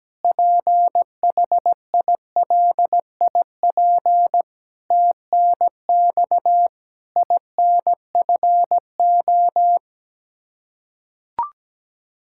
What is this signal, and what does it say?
Morse code, heard as a keyed tone, about 17 wpm, 700 Hz. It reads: PHILIP TNX INFO
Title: Practice QSO - Exchange 3 - More Info - 17wpm